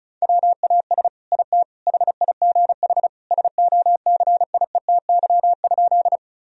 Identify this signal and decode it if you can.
Morse code, heard as a keyed tone, about 36 wpm, 700 Hz. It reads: WAS IT HIGH SOCIETY?
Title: Sentences from Top 1000 Words - Review Part 4 - 35wpm